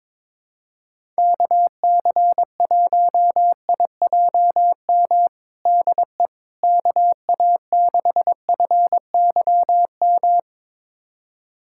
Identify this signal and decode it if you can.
Morse code, heard as a keyed tone, about 22 wpm, 700 Hz. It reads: KC1IJM DE KA6FYM
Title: USA Callsign DE Callsign - 6 Characters - 22wpm